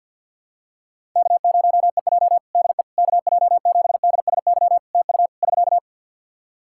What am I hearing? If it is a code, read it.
Morse code, heard as a keyed tone, about 50 wpm, 700 Hz. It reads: K0EJ DE KJ7DSY TU <SK>
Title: Practice QSO - Exchange 4 - Ending - 50wpm